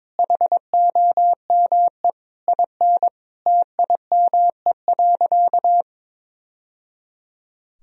Morse code, heard as a keyed tone, about 22 wpm, 700 Hz. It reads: HOME IN TIME.
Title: Sentences from Top 1000 Words - Review Part 3 - 22wpm